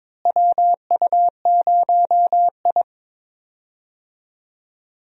Morse code, heard as a keyed tone, about 23 wpm, 700 Hz. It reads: WU0I
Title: USA Call Signs - 4 Characters - Repeated 3x - 22wpm